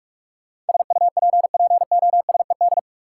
Morse code, heard as a keyed tone, about 46 wpm, 700 Hz. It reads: SUPPOSED
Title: Sets of 3 Words - New Word Drill - 45wpm